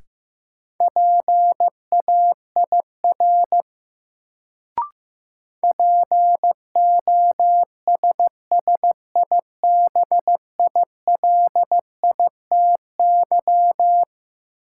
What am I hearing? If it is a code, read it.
Morse code, heard as a keyed tone, about 16 wpm, 700 Hz. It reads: PAIR POSSIBILITY
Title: Sets of 3 Words - New Word Drill - 15wpm